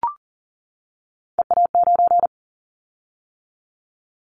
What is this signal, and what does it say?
Morse code, heard as a keyed tone, about 38 wpm, 700 Hz. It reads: EA9
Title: Callsign Prefixes for Top 100 DXCC - 40wpm